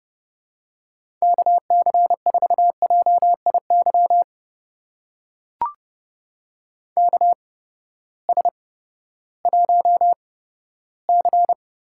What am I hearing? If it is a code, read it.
Morse code, heard as a keyed tone, about 32 wpm, spaced as for 15 wpm, 700 Hz. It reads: KC4JIY KS1C
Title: USA Call Signs - 8x Character Spacing - 30wpm